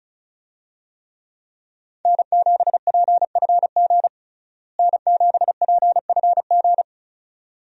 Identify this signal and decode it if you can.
Morse code, heard as a keyed tone, about 36 wpm, 700 Hz. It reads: N7PFG N7PFG
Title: USA Call Signs - 5 Characters - Repeated 2x - 35wpm